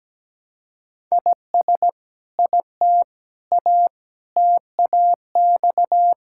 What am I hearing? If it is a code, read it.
Morse code, heard as a keyed tone, about 17 wpm, 700 Hz. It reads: IS IT A TAX
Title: Sentences from Top 1000 Words - Review Part 5 - 17wpm